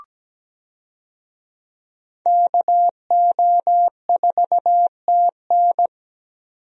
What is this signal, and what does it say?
Morse code, heard as a keyed tone, about 17 wpm, 700 Hz. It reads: KO4TN
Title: USA Call Signs - 5 Characters - 17wpm